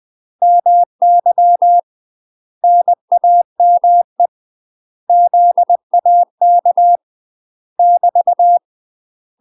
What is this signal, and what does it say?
Morse code, heard as a keyed tone, about 20 wpm, 700 Hz. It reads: MY NAME ZAK <BT>
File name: UK MASTER QSOs TRAINING / 20WPM / UK MASTER QSOs 0061